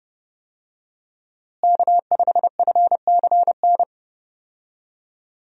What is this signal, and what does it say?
Morse code, heard as a keyed tone, about 30 wpm, 700 Hz. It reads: K5FCN
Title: USA Call Signs - 5 Characters - Repeated 3x - 30wpm